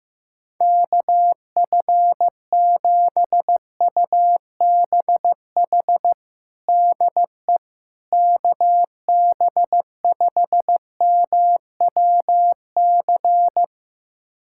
Morse code, heard as a keyed tone, about 15 wpm, 700 Hz. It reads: KF7UBH DE KB5MWC
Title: USA Callsign DE Callsign - 6 Characters - 15wpm